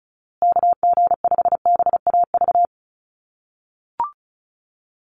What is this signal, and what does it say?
Morse code, heard as a keyed tone, about 34 wpm, 700 Hz. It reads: KG5BAV
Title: USA Call Signs Repeated 2x 35wpm